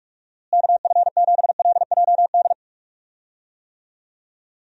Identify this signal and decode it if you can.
Morse code, heard as a keyed tone, about 46 wpm, 700 Hz. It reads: KF7LJD
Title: USA Call Signs - 6 Characters - Repeated 2x - 45wpm